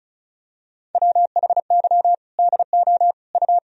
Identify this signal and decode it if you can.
Morse code, heard as a keyed tone, about 36 wpm, 700 Hz. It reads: WHY DO U
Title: Sentences from Top 400 Words and Abbreviations 35wpm